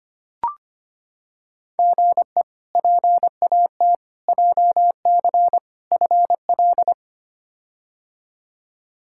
Morse code, heard as a keyed tone, about 25 wpm, 700 Hz. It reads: GE PAT JC FL
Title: K1USN SST QSO Practice - 25wpm